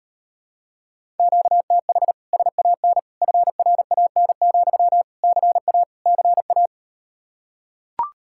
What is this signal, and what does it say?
Morse code, heard as a keyed tone, about 39 wpm, 700 Hz. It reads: QTH SAN FRAN, CA CA